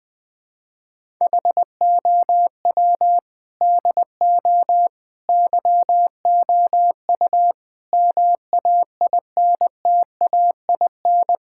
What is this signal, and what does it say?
Morse code, heard as a keyed tone, about 21 wpm, 700 Hz. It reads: HOW DO YOU MAINTAIN
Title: Sentences from Top 1000 Words - Review Part 6 - 20wpm